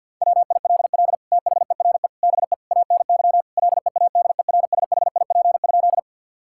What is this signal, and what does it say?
Morse code, heard as a keyed tone, about 50 wpm, 700 Hz. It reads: WILL THERE BE ANY LEADERSHIP?
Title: Sentences from Top 1600 Words 50wpm